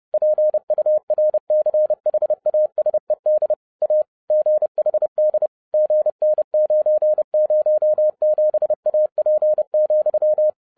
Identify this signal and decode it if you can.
Morse code, heard as a keyed tone, about 30 wpm, 610 Hz. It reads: PURCHASED A GHD GN907AP,